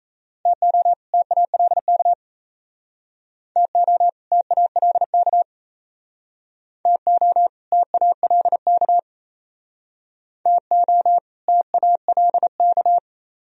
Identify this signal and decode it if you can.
Morse code, heard as a keyed tone, about 38 wpm, 700 Hz. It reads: TO TALK TO TALK TO TALK TO TALK